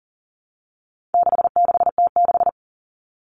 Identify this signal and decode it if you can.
Morse code, heard as a keyed tone, about 40 wpm, 700 Hz. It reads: 66T6
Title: Common Cut Numbers - 4-Digit 40wpm